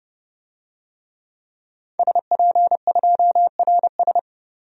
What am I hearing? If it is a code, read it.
Morse code, heard as a keyed tone, about 29 wpm, 700 Hz. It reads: SP2RS
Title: DXCC-100 Call Signs - 5 Characters - 30wpm